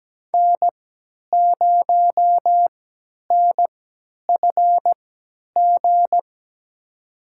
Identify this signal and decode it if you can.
Morse code, heard as a keyed tone, about 17 wpm, spaced as for 10 wpm, 700 Hz. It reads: N0NFG
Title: USA Call Signs - 5 Characters - 3x Character Spacing - 17wpm